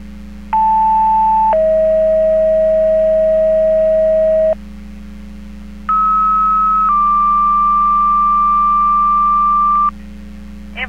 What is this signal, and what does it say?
Two-tone page, heard as a one-second tone, then a three-second tone, about 880 then 640 Hz, about 1290 then 1180 Hz, with speech. It is heard from 0.5 to 9.9 s.